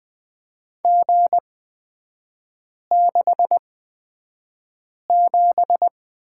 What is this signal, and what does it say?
Morse code, heard as a keyed tone, about 20 wpm, spaced as for 5 wpm, 690 Hz. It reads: G67
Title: 229 Lesson 37e - Letter x - Three Characters 20wpm